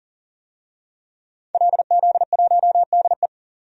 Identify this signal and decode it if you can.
Morse code, heard as a keyed tone, about 38 wpm, 700 Hz. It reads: LZ1DE
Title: DXCC-100 Call Signs - 5 Characters - 40wpm